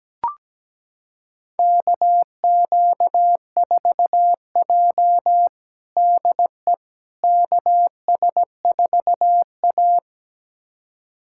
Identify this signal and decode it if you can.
Morse code, heard as a keyed tone, about 17 wpm, 700 Hz. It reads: KQ4J DE KS4A
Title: USA Callsign DE Callsign - 4 Characters - 17wpm